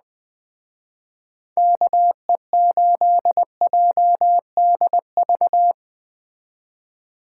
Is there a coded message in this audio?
Yes: Morse code from 1.6 to 5.7 s, heard as a keyed tone, about 20 wpm, 700 Hz. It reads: KE8JDV